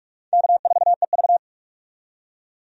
Morse code, heard as a keyed tone, about 46 wpm, 700 Hz. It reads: K4EV
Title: USA Call Signs - 4 Characters - 45wpm